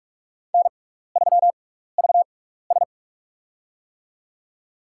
Morse code, heard as a keyed tone, about 46 wpm, spaced as for 16 wpm, 700 Hz. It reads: N3VS
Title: USA Call Signs - 4 Characters - 6x Character Spacing - 45wpm